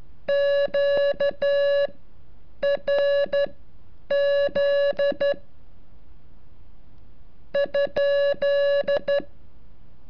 Morse code, heard as a keyed tone, about 12 wpm, spaced as for 8 wpm, 570 Hz. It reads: QRZ ?
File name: curso / ejercicio 34